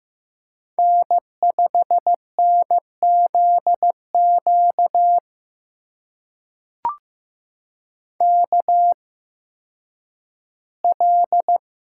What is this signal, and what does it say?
Morse code, heard as a keyed tone, about 15 wpm, spaced as for 10 wpm, 700 Hz. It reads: N5NZQ KL
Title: USA Call Signs - 5 Characters - 8x Character Spacing - 15wpm